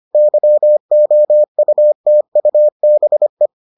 Morse code, heard as a keyed tone, about 25 wpm, 600 Hz. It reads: YOUTUBE